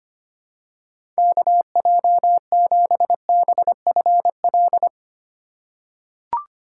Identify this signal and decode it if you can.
Morse code, heard as a keyed tone, about 25 wpm, 700 Hz. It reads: KJ7BFL
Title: USA Call Signs - 4x Character Spacing - 25wpm